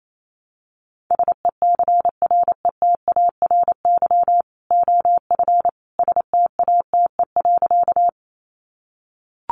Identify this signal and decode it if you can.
Morse code, heard as a keyed tone, about 28 wpm, 700 Hz. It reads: SECRETARY OF STATE.
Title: Sentences from Top 1300 Words 28wpm